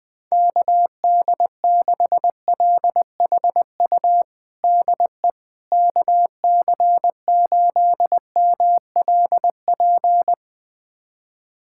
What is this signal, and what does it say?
Morse code, heard as a keyed tone, about 20 wpm, 700 Hz. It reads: KD6LHU DE KC8MLP
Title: USA Callsign DE Callsign 20wpm